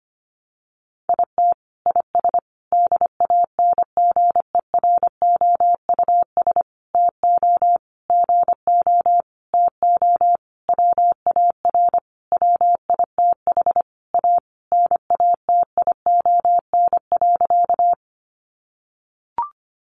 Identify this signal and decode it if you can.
Morse code, heard as a keyed tone, about 25 wpm, 700 Hz. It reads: IT IS DANGEROUS TO GO TO WAR WITH A NATION.
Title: Sentences from Top 1200 Words 25wpm